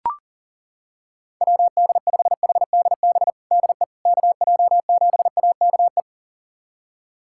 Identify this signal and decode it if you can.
Morse code, heard as a keyed tone, about 37 wpm, 700 Hz. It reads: WD5HDB DE KJ7AKE
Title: USA Callsign DE Callsign - 6 Characters - 40wpm